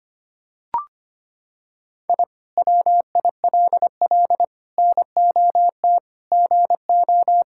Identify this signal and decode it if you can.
Morse code, heard as a keyed tone, about 25 wpm, 700 Hz. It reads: I WILL NOT GO
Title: Sentences from Top 1000 Words - Review Part 6 - 25wpm